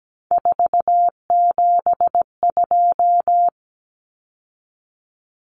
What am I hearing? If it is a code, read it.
Morse code, heard as a keyed tone, about 17 wpm, 700 Hz. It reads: 472
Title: Numbers - 3-Digit 17wpm